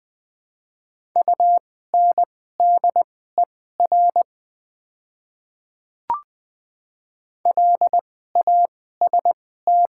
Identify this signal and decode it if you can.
Morse code, heard as a keyed tone, about 20 wpm, spaced as for 14 wpm, 700 Hz. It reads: UNDER LAST
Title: 2x Character Spacing - Top 300 Words - 20wpm